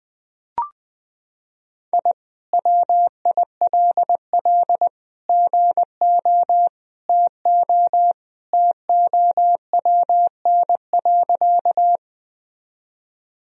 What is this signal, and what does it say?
Morse code, heard as a keyed tone, about 20 wpm, 700 Hz. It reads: I WILL GO TO TOWN.
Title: Sentences from Top 1000 Words - Review Part 2 - 20wpm